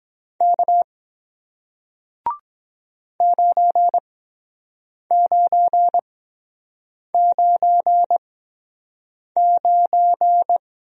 Morse code, heard as a keyed tone, about 23 wpm, spaced as for 7 wpm, 700 Hz. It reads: K 9999